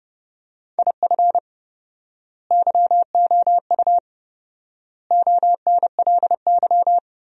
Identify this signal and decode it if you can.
Morse code, heard as a keyed tone, about 32 wpm, 700 Hz. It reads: IF YOU ONLY
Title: 4x Word-Spacing - Sentences from Top 400 Words - 30wpm